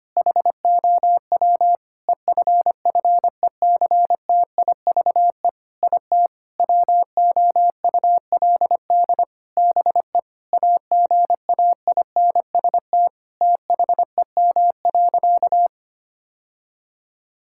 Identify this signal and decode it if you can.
Morse code, heard as a keyed tone, about 25 wpm, 700 Hz. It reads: HOW EFFECTIVE IT WOULD BE AGAINST THEM.
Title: Sentences from Top 1000 Words - Review Part 6 - 25wpm